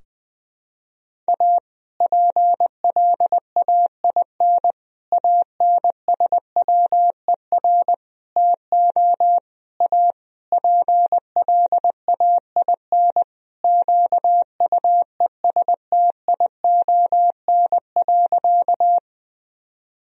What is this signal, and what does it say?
Morse code, heard as a keyed tone, about 21 wpm, 700 Hz. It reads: A PLAIN ANSWER TO A PLAIN QUESTION.
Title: Sentences from Top 1000 Words - Review Part 2 - 20wpm